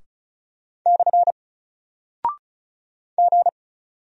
Morse code, heard as a keyed tone, about 36 wpm, 700 Hz. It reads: / G
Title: Single Character (Letter:Number:Punctuation:Prosign) Alt 35wpm